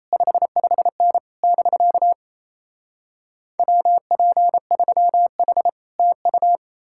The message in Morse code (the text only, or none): 55N <BK> WP3H TU